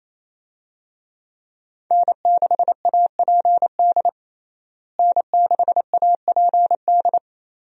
Morse code, heard as a keyed tone, about 28 wpm, 700 Hz. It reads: N6APD N6APD
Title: USA Call Signs - 5 Characters - Repeated 2x - 28wpm